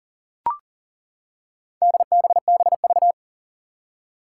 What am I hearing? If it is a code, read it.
Morse code, heard as a keyed tone, about 40 wpm, 700 Hz. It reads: DBBV
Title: Four Challenging Letter-Number Combinations 40wpm